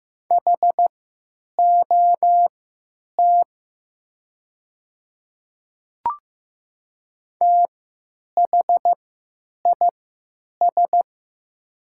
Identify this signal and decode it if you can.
Morse code, heard as a keyed tone, about 15 wpm, spaced as for 9 wpm, 700 Hz. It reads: HOT THIS
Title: 3x Character Spacing - Top 100 Words - 15wpm